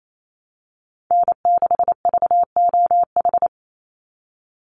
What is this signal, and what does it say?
Morse code, heard as a keyed tone, about 28 wpm, 700 Hz. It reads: N6VOH
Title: Call Signs Repeated 3x - 28wpm